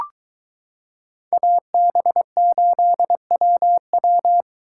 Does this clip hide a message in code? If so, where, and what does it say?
Yes: Morse code from 1.3 to 4.4 s, heard as a keyed tone, about 23 wpm, 700 Hz. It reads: AB8WW